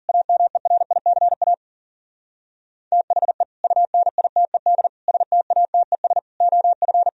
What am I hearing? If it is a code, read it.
Morse code, heard as a keyed tone, about 52 wpm, 700 Hz. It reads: AMERICA THE UNITED STATES OF